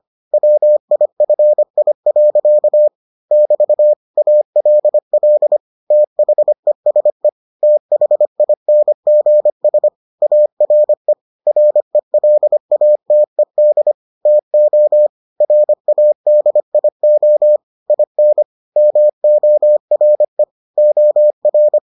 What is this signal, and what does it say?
Morse code, heard as a keyed tone, about 25 wpm, 600 Hz. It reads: WIFI. <BT> ALL THESE THINGS ARE RELATED TO RADIO IN MORE OR